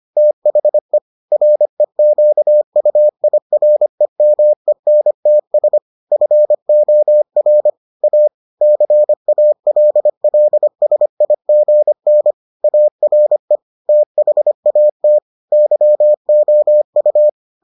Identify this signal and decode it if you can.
Morse code, heard as a keyed tone, about 25 wpm, 600 Hz. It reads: THE REQUIREMENTS FOR A CALLSIGN ARE THAT YOU